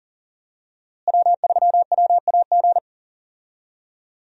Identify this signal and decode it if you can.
Morse code, heard as a keyed tone, about 40 wpm, 700 Hz. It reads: W3WAG
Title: USA Call Signs - 5 Characters - Repeated 2x - 40wpm